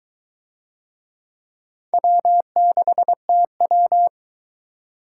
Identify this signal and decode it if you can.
Morse code, heard as a keyed tone, about 22 wpm, 700 Hz. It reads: W6TW